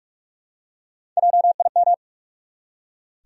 Morse code, heard as a keyed tone, about 43 wpm, 700 Hz. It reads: JIM
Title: USA Common Names - Max 4 Letters - 45wpm